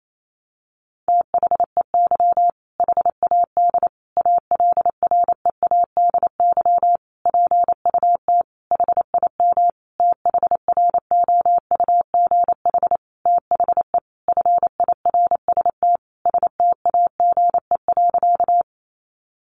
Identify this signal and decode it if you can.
Morse code, heard as a keyed tone, about 28 wpm, 700 Hz. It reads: THEY HAD ALREADY PUT HIM THROUGH THE FIRST STAGE.
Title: Sentences from Top 1200 Words 28wpm